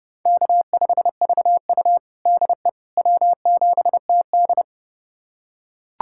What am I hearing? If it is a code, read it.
Morse code, heard as a keyed tone, about 30 wpm, 700 Hz. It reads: K5VU DE W7TD
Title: USA Callsign DE Callsign - 4 Characters - 30wpm